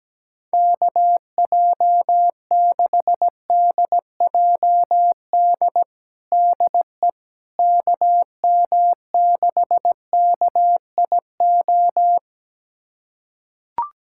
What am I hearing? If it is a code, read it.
Morse code, heard as a keyed tone, about 17 wpm, 700 Hz. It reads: KJ6DJD DE KM6KIO
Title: USA Callsign DE Callsign 17wpm